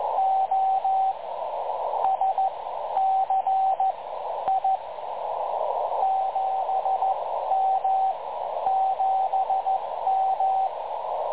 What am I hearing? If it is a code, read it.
Morse code, heard as a keyed tone, about 14 wpm, spaced as for 10 wpm, 750 Hz, with radio noise. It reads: OSC I 7M 7M
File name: dlazaawansowanych / nagraniazpasma / websdr recording 3565.0kHz